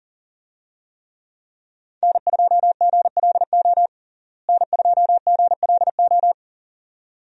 Morse code, heard as a keyed tone, about 40 wpm, 700 Hz. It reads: N2GLO N2GLO
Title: USA Call Signs - 5 Characters - Repeated 2x - 40wpm